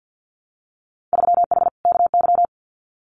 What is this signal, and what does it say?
Morse code, heard as a keyed tone, about 50 wpm, 700 Hz. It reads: 3H KY
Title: Winter Field Day Exchanges Encore 50wpm